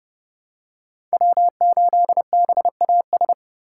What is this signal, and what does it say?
Morse code, heard as a keyed tone, about 30 wpm, 700 Hz. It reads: W8BAS
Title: USA Call Signs - 5 Characters - 3x Character Spacing - 30wpm